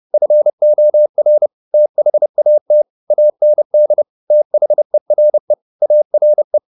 Morse code, heard as a keyed tone, about 30 wpm, 600 Hz. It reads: FOR THAT AND THERE ARE